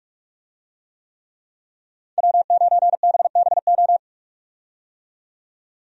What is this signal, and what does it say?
Morse code, heard as a keyed tone, about 46 wpm, 700 Hz. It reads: W9BBO
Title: USA Call Signs - 5 Characters - 45wpm